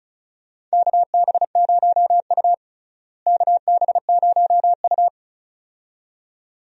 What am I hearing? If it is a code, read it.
Morse code, heard as a keyed tone, about 34 wpm, 700 Hz. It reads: KB0U KB0U
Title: USA Call Signs - 4 Characters - Repeated 2x - 35wpm